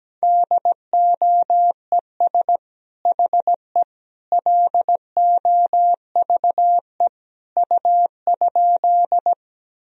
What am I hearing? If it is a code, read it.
Morse code, heard as a keyed tone, about 17 wpm, 700 Hz. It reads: DOES HE LOVE U?
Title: Sentences from Top 400 Words and Abbreviations 17wpm